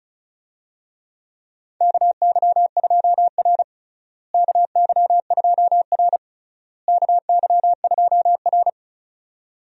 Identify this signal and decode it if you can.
Morse code, heard as a keyed tone, about 34 wpm, 700 Hz. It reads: KY2R KY2R KY2R
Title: USA Call Signs - 4 Characters - Repeated 3x - 35wpm